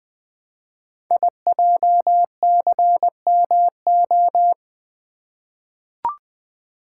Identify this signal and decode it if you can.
Morse code, heard as a keyed tone, about 20 wpm, 700 Hz. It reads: IJCMO
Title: Code Groups - 5 Letters - 20wpm